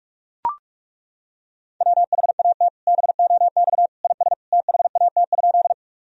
Morse code, heard as a keyed tone, about 41 wpm, 700 Hz. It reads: WHAT BOX IS THAT?
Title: Sentences from Top 1000 Words - Review Part 2 - 45wpm